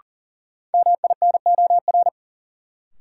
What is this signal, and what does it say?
Morse code, heard as a keyed tone, about 40 wpm, 700 Hz. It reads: MINOR
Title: Sentences from Top 1500 Words - New Word Drill - 40wpm